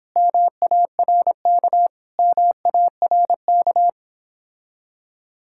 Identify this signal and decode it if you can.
Morse code, heard as a keyed tone, about 31 wpm, 700 Hz. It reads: MARK MARK